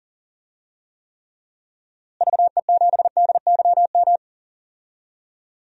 Morse code, heard as a keyed tone, about 40 wpm, 700 Hz. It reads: VE7DYM
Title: DXCC-100 Call Signs - 6 Characters - Repeated 2x - 40wpm